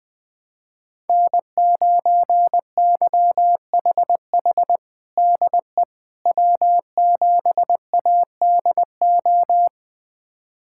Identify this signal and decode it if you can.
Morse code, heard as a keyed tone, about 21 wpm, 700 Hz. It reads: N9YHH DE W7ADO
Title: USA Callsign DE Callsign - 5 Characters - 20wpm